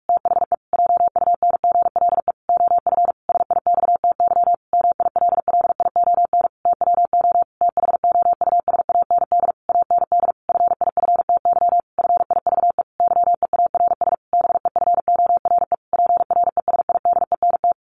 Morse code, heard as a keyed tone, about 44 wpm, 700 Hz. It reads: THE JUNGLE OF SIXTY MILLION TWO THOUSAND AND FIFTY FIVE YEARS BEFORE PRESIDENT
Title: A Sound Of Thunder 45wpm